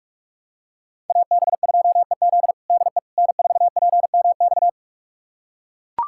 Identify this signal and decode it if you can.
Morse code, heard as a keyed tone, about 41 wpm, 700 Hz. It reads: AB2EZ DE N4PMX
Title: USA Callsign DE Callsign - 5 Characters - 45wpm